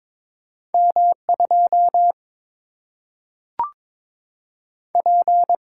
Morse code, heard as a keyed tone, about 22 wpm, 700 Hz. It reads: M 2 P
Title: Single-Letter Single-Number 22wpm